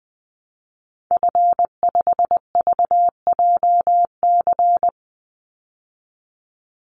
Morse code, heard as a keyed tone, about 19 wpm, 700 Hz. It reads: F5VJC
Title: Call Signs Repeated 3x - 20wpm